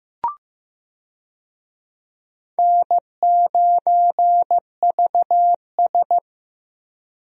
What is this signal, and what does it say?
Morse code, heard as a keyed tone, about 15 wpm, 700 Hz. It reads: N9VS
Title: USA Call Signs - 4 Characters - Repeated 2x - 15wpm